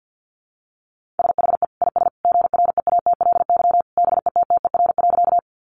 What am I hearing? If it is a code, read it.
Morse code, heard as a keyed tone, about 50 wpm, 700 Hz. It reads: SHE IS GREATLY BETTER.
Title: Sentences from Top 1400 Words 50wpm